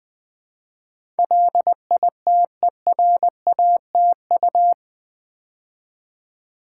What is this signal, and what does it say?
Morse code, heard as a keyed tone, about 20 wpm, 700 Hz. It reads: LITERATU
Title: Build A Word - 4 Letters At A Time - 10-Letter Words - 20wpm